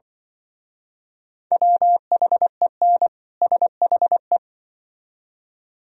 Morse code, heard as a keyed tone, about 25 wpm, 700 Hz. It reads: WHEN SHE
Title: Speed-Racing - Sets of 2 Words 20wpm